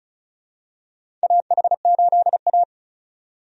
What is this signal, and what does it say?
Morse code, heard as a keyed tone, about 36 wpm, 700 Hz. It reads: AH8A
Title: USA Call Signs - 4 Characters - 35wpm